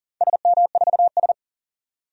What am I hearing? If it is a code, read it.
Morse code, heard as a keyed tone, about 39 wpm, 700 Hz. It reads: SM4S
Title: DXCC-100 Call Signs - 4 Characters - 40wpm